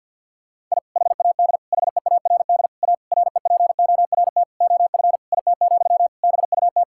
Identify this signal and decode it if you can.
Morse code, heard as a keyed tone, about 50 wpm, 700 Hz. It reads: I HAD HEARD A REPORT OF IT, BUT
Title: Sentences from Top 1000 Words - Review Part 6 - 50wpm